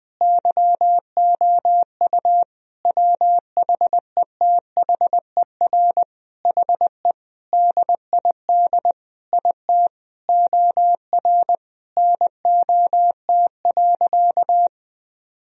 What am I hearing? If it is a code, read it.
Morse code, heard as a keyed tone, about 20 wpm, 700 Hz. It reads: YOU WHETHER HE DID IT OR NOT.
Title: Sentences from Top 1000 Words - Review Part 3 - 20wpm